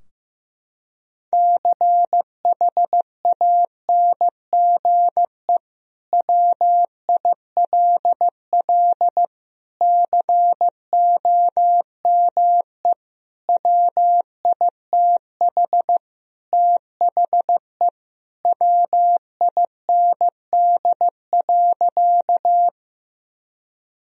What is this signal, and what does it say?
Morse code, heard as a keyed tone, about 15 wpm, 700 Hz. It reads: CHANGE WILL COME WITH THE WIND.
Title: Sentences from Top 1000 Words - Review Part 2 - 15wpm